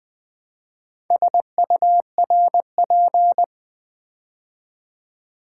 Morse code, heard as a keyed tone, about 20 wpm, 690 Hz. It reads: SURP
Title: Build A Word - 2 Letters At A Time - 8-Letter Words - 20wpm